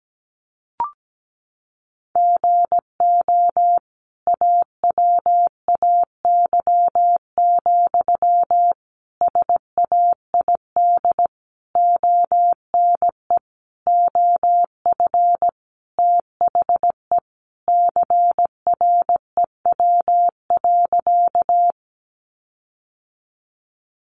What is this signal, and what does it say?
Morse code, heard as a keyed tone, about 17 wpm, 700 Hz. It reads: GO AWAY, SAID ONE OF THE CREW.
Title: Sentences from Top 1500 Words 17wpm